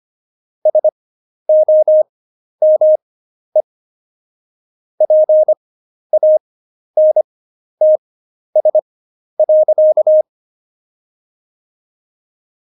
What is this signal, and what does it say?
Morse code, heard as a keyed tone, about 25 wpm, spaced as for 11 wpm, 620 Hz. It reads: SOME PANTS.
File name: Jokes-25-char-10-eff